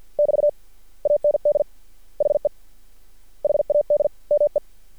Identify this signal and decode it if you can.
Morse code, heard as a keyed tone, about 50 wpm, 590 Hz. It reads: <BT> AND HE HAD ME
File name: archived / CRSnet